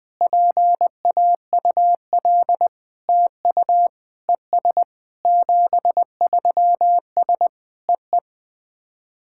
Morse code, heard as a keyed tone, about 20 wpm, 700 Hz. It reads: PAUL TU ES 73S EE
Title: SOTA QSO Practice 20wpm